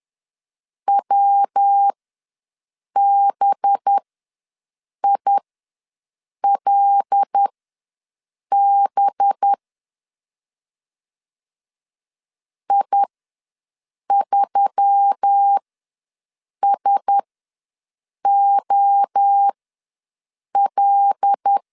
Morse code, heard as a keyed tone, about 10 wpm, spaced as for 6 wpm, 800 Hz. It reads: WBILB I3SOL
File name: LEZIONI / 5 LBVC / 5 LB ws 10wpm 3616